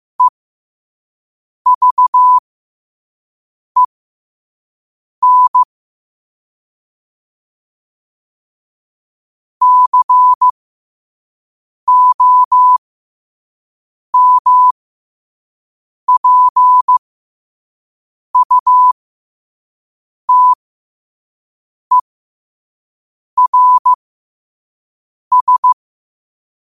Morse code, heard as a keyed tone, about 15 wpm, spaced as for 5 wpm, 1000 Hz. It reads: EVEN COMPUTERS
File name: qotdfiles / 5wpm